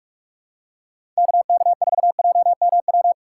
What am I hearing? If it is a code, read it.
Morse code, heard as a keyed tone, about 46 wpm, 690 Hz. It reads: KK4JMW
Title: USA Call Signs - 6 Characters - 2x Character Spacing - 45wpm